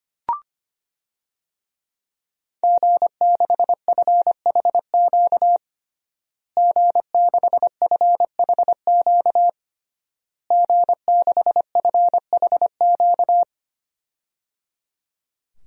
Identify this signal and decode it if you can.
Morse code, heard as a keyed tone, about 25 wpm, 700 Hz. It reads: G6FHQ G6FHQ G6FHQ
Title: DXCC-100 Call Signs - 5 Characters - Repeated 3x - 25wpm